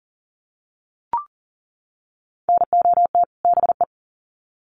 Morse code, heard as a keyed tone, about 40 wpm, 700 Hz. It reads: NOT BE
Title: Sets of 2 Words 40wpm